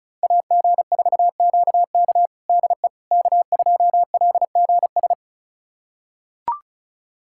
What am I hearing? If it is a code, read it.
Morse code, heard as a keyed tone, about 34 wpm, 700 Hz. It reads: AG4QK DE K2LGS
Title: USA Callsign DE Callsign - 5 Characters - 35wpm